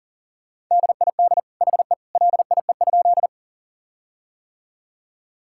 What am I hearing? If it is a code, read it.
Morse code, heard as a keyed tone, about 40 wpm, 700 Hz. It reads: DID HE LIE?
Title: Sentences from Top 1600 Words 40wpm